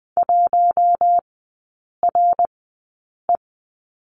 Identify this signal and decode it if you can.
Morse code, heard as a keyed tone, about 20 wpm, spaced as for 8 wpm, 700 Hz. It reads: 1RE
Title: 093 Lesson 16e - Letter c - Three Characters 20wpm